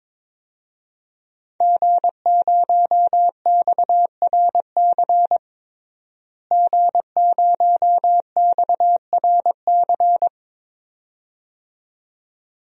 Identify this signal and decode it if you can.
Morse code, heard as a keyed tone, about 22 wpm, 700 Hz. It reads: G0XRC G0XRC
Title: DXCC-100 Call Signs - 5 Characters - Repeated 2x - 22wpm